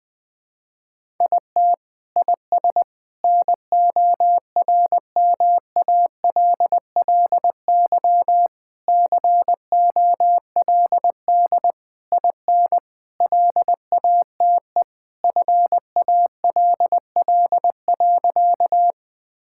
Morse code, heard as a keyed tone, about 20 wpm, 700 Hz. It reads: IT IS NORMALLY COLD IN LATE FALL.
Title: Sentences from Top 1000 Words - Review Part 6 - 20wpm